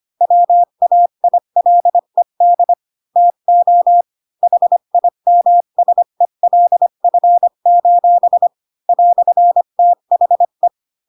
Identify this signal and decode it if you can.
Morse code, heard as a keyed tone, about 25 wpm, 700 Hz. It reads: WAILED TO HIMSELF: "THE
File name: Pinocchio / Pinocchio ch5 20 25 ik1hge